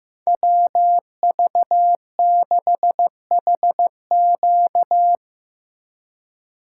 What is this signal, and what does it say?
Morse code, heard as a keyed tone, about 16 wpm, 700 Hz. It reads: WV6HQ
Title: USA Call Signs - 5 Characters - 4x Character Spacing - 15wpm